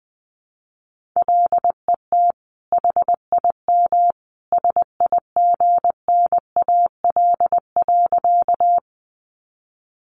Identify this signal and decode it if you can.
Morse code, heard as a keyed tone, about 20 wpm, 700 Hz. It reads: LET HIM SIGNAL.
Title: Sentences from Top 1400 Words 20wpm